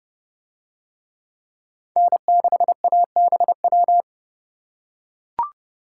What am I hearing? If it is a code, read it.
Morse code, heard as a keyed tone, about 30 wpm, 700 Hz. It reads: N6ABW